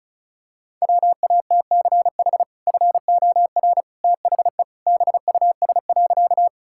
Morse code, heard as a keyed tone, about 34 wpm, 700 Hz. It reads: WATCH FOR THE BUS.
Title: Sentences from Top 1000 Words - Review Part 6 - 35wpm